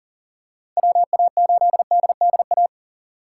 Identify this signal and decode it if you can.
Morse code, heard as a keyed tone, about 40 wpm, 700 Hz. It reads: WA8DDA
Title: USA Call Signs - 6 Characters - 3x Character Spacing - 40wpm